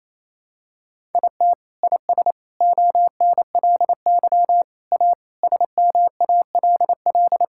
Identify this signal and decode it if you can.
Morse code, heard as a keyed tone, about 28 wpm, 700 Hz. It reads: IT IS ONLY A SMALL
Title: Sentences from Top 1000 Words - Review Part 5 - 28wpm